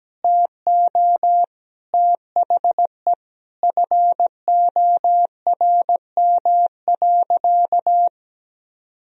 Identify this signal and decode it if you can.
Morse code, heard as a keyed tone, about 17 wpm, 700 Hz. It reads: TO THE FORM.
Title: Sentences from Top 1000 Words - Review Part 1 - 17wpm